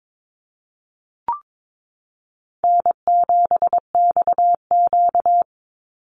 Morse code, heard as a keyed tone, about 22 wpm, 700 Hz. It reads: N7XQ
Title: USA Call Signs 22wpm